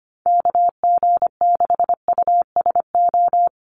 Morse code, heard as a keyed tone, about 25 wpm, 700 Hz. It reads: KG6USO
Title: USA Call Signs Repeated 3x 25wpm